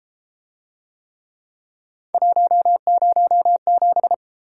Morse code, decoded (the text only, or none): 107